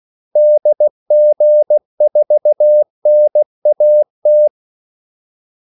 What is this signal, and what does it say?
Morse code, heard as a keyed tone, about 16 wpm, 600 Hz. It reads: DG4NAT
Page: Kaldesignaler 16 wpm